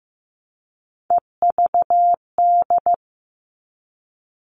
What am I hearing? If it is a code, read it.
Morse code, heard as a keyed tone, about 15 wpm, 700 Hz. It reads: EVD
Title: All Cut Numbers - 3-Digit 15wpm